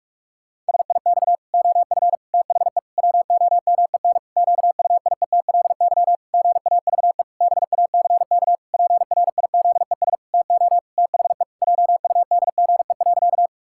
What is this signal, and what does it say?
Morse code, heard as a keyed tone, about 46 wpm, 700 Hz. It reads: SIX OF THE WOMEN QUIETLY GAVE BACK PRIZES TO THE JUDGE.
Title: Pangram Challenge v2 - 45wpm